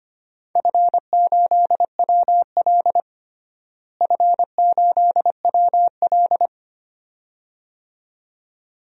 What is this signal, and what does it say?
Morse code, heard as a keyed tone, about 25 wpm, 700 Hz. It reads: F8WL F8WL
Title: DXCC-100 Call Signs - 4 Characters - Repeated 2x - 25wpm